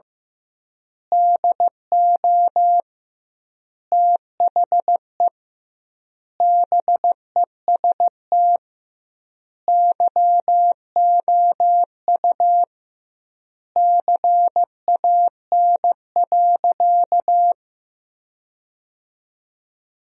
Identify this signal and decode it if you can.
Morse code, heard as a keyed tone, about 16 wpm, 700 Hz. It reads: DO THE BEST YOU CAN.
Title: 2x Word-Spacing - Sentences from Top 500 Words - 15wpm